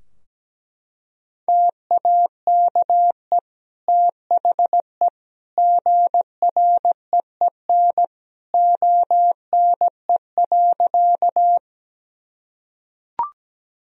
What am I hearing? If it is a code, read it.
Morse code, heard as a keyed tone, about 17 wpm, 700 Hz. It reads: TAKE THE GREEN ONE.
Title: Sentences from Top 1000 Words - Review Part 2 - 17wpm